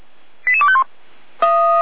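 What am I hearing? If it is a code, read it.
Five-tone selective call (ZVEI), heard as quick runs of five tones, tones of about 70 ms. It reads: 88251
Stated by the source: ZVEI five-tone selective call received on an SDR.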